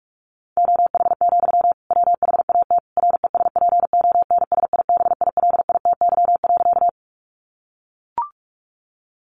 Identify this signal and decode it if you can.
Morse code, heard as a keyed tone, about 44 wpm, 700 Hz. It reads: OH, WHAT RESPONSIBILITY.
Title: Sentences from Top 1200 Words 45wpm